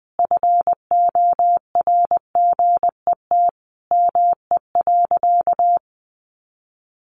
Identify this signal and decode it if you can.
Morse code, heard as a keyed tone, about 20 wpm, 700 Hz. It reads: FORGET ME.
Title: Sentences from Top 1200 Words 20wpm